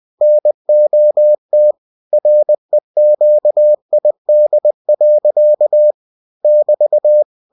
Morse code, heard as a keyed tone, about 20 wpm, 600 Hz. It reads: NOT REQID. <BT>
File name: cw-books / winnie-the-pooh / winnie-20wpm-page-0029